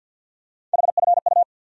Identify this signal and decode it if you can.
Morse code, heard as a keyed tone, about 50 wpm, 700 Hz. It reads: HFU
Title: USA Call Sign Postfixes - 3 Letters 50wpm